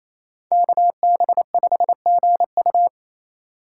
Morse code, heard as a keyed tone, about 28 wpm, 700 Hz. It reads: KB5GU
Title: USA Call Signs - 5 Characters - 2x Character Spacing - 28wpm